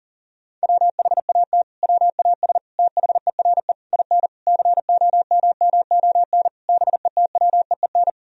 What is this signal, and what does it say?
Morse code, heard as a keyed tone, about 40 wpm, 700 Hz. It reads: WHAT WAS THERE IN COMMON BETWEEN
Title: Sentences from Top 1000 Words - Review Part 3 - 40wpm